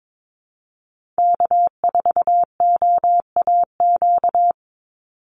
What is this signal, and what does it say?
Morse code, heard as a keyed tone, about 22 wpm, 690 Hz. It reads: K4OAQ
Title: Call Signs Repeated 3x - 22wpm